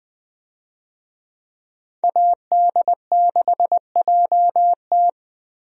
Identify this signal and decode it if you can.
Morse code, heard as a keyed tone, about 20 wpm, 700 Hz. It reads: AD6JT
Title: USA Call Signs - 5 Characters - Repeated 2x - 20wpm